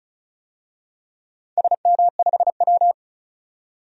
Morse code, heard as a keyed tone, about 36 wpm, 700 Hz. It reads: SM5W
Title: DXCC-100 Call Signs - 4 Characters - 35wpm